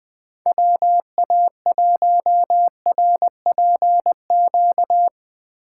Morse code, heard as a keyed tone, about 21 wpm, 700 Hz. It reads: WA1RPQ
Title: Speed Racing - USA 2x3 Call Signs 17wpm